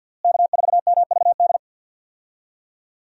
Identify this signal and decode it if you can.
Morse code, heard as a keyed tone, about 51 wpm, 700 Hz. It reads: K4MVD
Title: USA Call Signs - 5 Characters - Repeated 2x - 50wpm